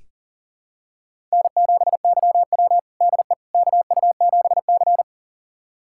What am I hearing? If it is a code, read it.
Morse code, heard as a keyed tone, about 38 wpm, 700 Hz. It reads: N7YW DE KU7C
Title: USA Callsign DE Callsign - 4 Characters - 40wpm